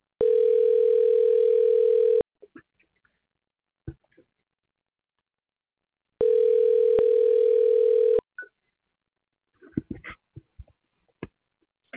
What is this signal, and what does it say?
Ringback tone (North America), 440 + 480 Hz.